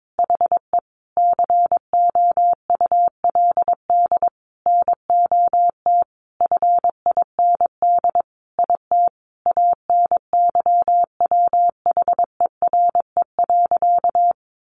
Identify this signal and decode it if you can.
Morse code, heard as a keyed tone, about 22 wpm, 700 Hz. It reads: HE COULD NOT FIND IT ANYWHERE.
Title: Sentences from Top 1400 Words 22wpm